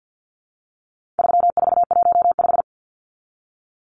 Morse code, heard as a keyed tone, about 49 wpm, 700 Hz. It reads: 3415
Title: Numbers - 4-Digit 50wpm